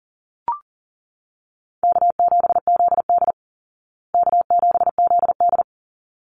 Morse code, heard as a keyed tone, about 40 wpm, 700 Hz. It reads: K7ZD K7ZD
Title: Easy Call Signs Repeated 2x - 40wpm